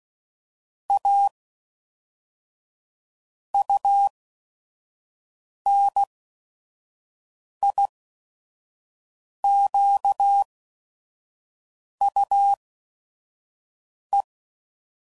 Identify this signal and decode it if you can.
Morse code, heard as a keyed tone, about 16 wpm, spaced as for 5 wpm, 800 Hz. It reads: A UNIQUE
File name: CW Code Groups @5WPM / Technical Subjects / Gates Intro